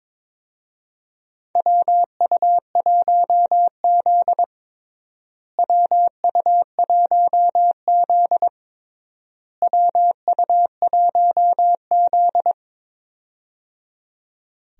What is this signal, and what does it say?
Morse code, heard as a keyed tone, about 22 wpm, 700 Hz. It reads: WU1Z WU1Z WU1Z
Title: USA Call Signs - 4 Characters - Repeated 3x - 22wpm